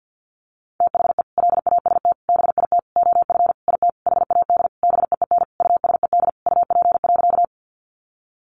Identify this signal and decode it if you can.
Morse code, heard as a keyed tone, about 50 wpm, 700 Hz. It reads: THE LAST BIT OF IT HAD BEEN USED UP.
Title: Sentences from Top 900 Words 50wpm